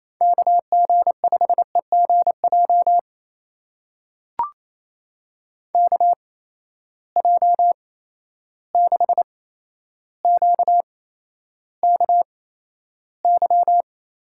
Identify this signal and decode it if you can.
Morse code, heard as a keyed tone, about 28 wpm, spaced as for 12 wpm, 700 Hz. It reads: KG5EGJ KJ6QKY
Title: USA Call Signs - 6 Characters - 8x Character Spacing - 28wpm